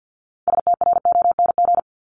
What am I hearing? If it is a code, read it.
Morse code, heard as a keyed tone, about 50 wpm, 700 Hz. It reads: STRONG
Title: Top 500 Words 50wpm